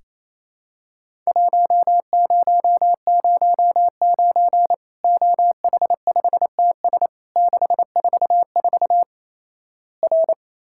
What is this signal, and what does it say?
Morse code, heard as a keyed tone, about 28 wpm, 700 Hz. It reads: 1009 OH5TS 644 R
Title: Worked All Europe DX Contest - QTC Exchange - 28wpm